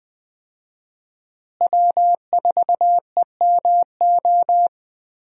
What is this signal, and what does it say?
Morse code, heard as a keyed tone, about 20 wpm, 700 Hz. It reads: W4EMO
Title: USA Call Signs - 5 Characters - Repeated 2x - 20wpm